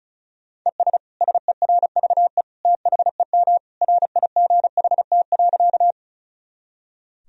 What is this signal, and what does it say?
Morse code, heard as a keyed tone, about 36 wpm, 700 Hz. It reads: ES SERVE THEM RIGHT.
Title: Sentences from Top 400 Words and Abbreviations 35wpm